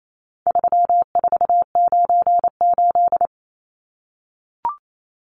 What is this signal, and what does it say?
Morse code, heard as a keyed tone, about 28 wpm, 700 Hz. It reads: 3498
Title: Numbers - 4-Digit 28wpm